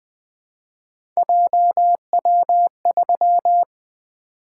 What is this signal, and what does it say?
Morse code, heard as a keyed tone, about 20 wpm, 700 Hz. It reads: JW3
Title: Three Letter-Number Combinations - 20wpm